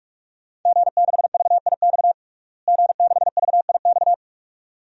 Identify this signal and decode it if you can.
Morse code, heard as a keyed tone, about 42 wpm, 700 Hz. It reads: G6VIX G6VIX
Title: DXCC-100 Call Signs - 5 Characters - Repeated 2x - 45wpm